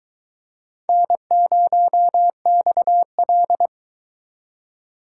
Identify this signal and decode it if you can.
Morse code, heard as a keyed tone, about 22 wpm, 700 Hz. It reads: N0XL